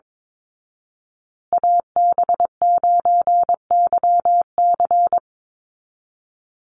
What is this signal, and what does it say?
Morse code, heard as a keyed tone, about 22 wpm, 700 Hz. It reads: AB9YC
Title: Call Signs Repeated 2x - 22wpm